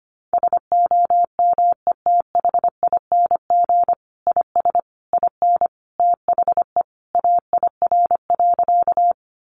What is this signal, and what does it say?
Morse code, heard as a keyed tone, about 25 wpm, 700 Hz. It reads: SOMETHING IS IN THE AIR.
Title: Sentences from Top 600 Words 25wpm